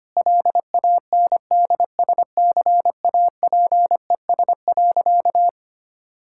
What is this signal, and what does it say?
Morse code, heard as a keyed tone, about 25 wpm, 690 Hz. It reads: LANDSCAPES.
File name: Easy Sentences with Hints - 25wpm